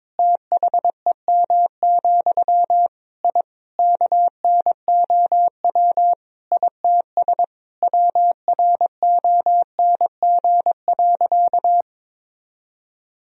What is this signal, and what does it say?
Morse code, heard as a keyed tone, about 22 wpm, 700 Hz. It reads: THEM, I KNOW ITS WRONG.
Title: Sentences from Top 1000 Words - Review Part 1 - 22wpm